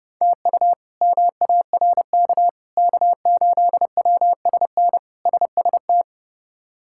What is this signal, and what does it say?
Morse code, heard as a keyed tone, about 30 wpm, 700 Hz. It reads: TU MARK K8WSN SST
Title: K1USN SST QSO Practice - 30wpm